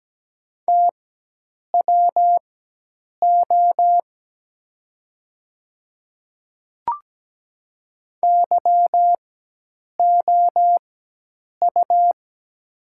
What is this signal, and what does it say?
Morse code, heard as a keyed tone, about 17 wpm, spaced as for 8 wpm, 700 Hz. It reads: TWO YOU
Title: 4x Character Spacing - Top 100 Words - 17wpm